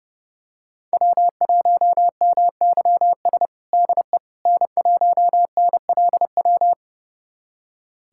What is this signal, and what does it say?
Morse code, heard as a keyed tone, about 30 wpm, 700 Hz. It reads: W1MYS DE N1NLW
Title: USA Callsign DE Callsign - 5 Characters - 30wpm